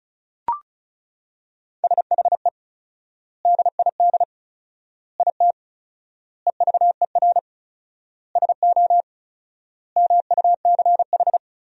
Morse code, heard as a keyed tone, about 36 wpm, 700 Hz. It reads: SHE DID IT EVER SO MUCH
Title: 4x Word-Spacing - Sentences from Top 400 Words - 35wpm